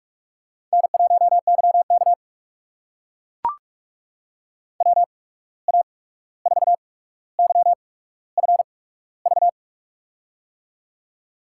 Morse code, heard as a keyed tone, about 44 wpm, spaced as for 17 wpm, 700 Hz. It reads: N1YK WA4YFV
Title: USA Call Signs - 8x Character Spacing - 45wpm